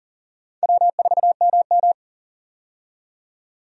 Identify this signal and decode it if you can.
Morse code, heard as a keyed tone, about 40 wpm, 700 Hz. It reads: W4MM
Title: USA Call Signs - 4 Characters - 40wpm